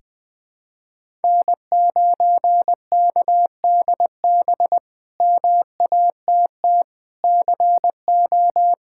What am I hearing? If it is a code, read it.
Morse code, heard as a keyed tone, about 21 wpm, 700 Hz. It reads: N9KDB MATT CO
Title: K1USN SST QSO Practice - 20wpm